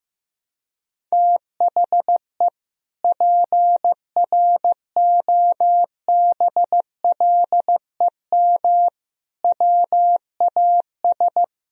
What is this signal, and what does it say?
Morse code, heard as a keyed tone, about 15 wpm, 700 Hz. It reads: THE PROBLEM WAS
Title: Sentences from Top 1000 Words - Review Part 2 - 15wpm